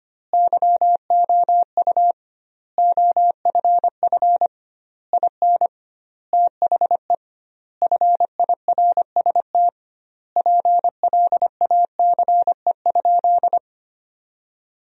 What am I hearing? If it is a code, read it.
Morse code, heard as a keyed tone, about 25 wpm, 700 Hz. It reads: YOU OFF IN THE FIRST PLACE?
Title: 2x Word-Spacing - Sentences from Top 200 Words - 25wpm